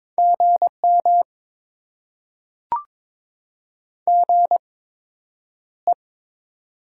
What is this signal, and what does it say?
Morse code, heard as a keyed tone, about 22 wpm, spaced as for 9 wpm, 700 Hz. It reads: GM GE
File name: Top 20 QSO Elements - 8x Character Spacing - 22wpm